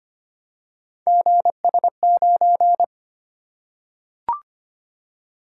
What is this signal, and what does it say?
Morse code, heard as a keyed tone, about 25 wpm, 700 Hz. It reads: GS9
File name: Three Letter-Number Combinations - 25wpm